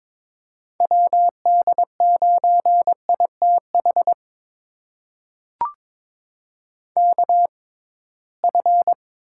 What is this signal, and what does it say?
Morse code, heard as a keyed tone, about 22 wpm, spaced as for 17 wpm, 700 Hz. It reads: WD9ITH KF
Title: USA Call Signs - 6 Characters - 6x Character Spacing - 22wpm